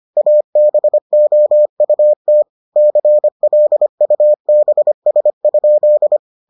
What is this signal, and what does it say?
Morse code, heard as a keyed tone, about 25 wpm, 600 Hz. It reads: ABOUT CLUBS?